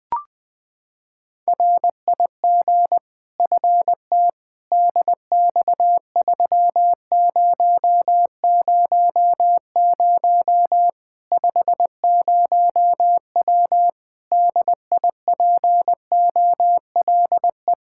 Morse code, heard as a keyed tone, about 20 wpm, 700 Hz. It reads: RIG FT DX3000 50W DIPOLE
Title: Practice QSO - Exchange 2 - Info - 20wpm